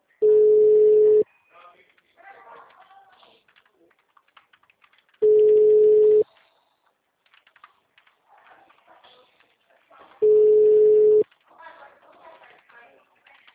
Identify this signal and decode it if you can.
Ringback tone (Europe, 425 Hz).